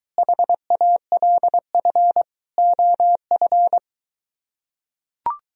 Morse code, heard as a keyed tone, about 23 wpm, 700 Hz. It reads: HALF OF